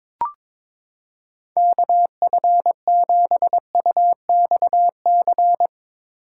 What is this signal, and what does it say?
Morse code, heard as a keyed tone, about 22 wpm, 700 Hz. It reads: KF7UXC
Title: K1USN SST QSO Practice - 22wpm